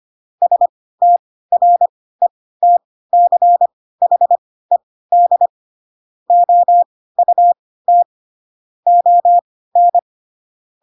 Morse code, heard as a keyed tone, about 25 wpm, spaced as for 16 wpm, 700 Hz. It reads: STRETCHED OUT ON
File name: Pinocchio / Pinocchio ch7 10 25 ik1hge